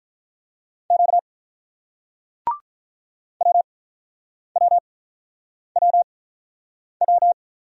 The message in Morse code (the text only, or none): Q WWWW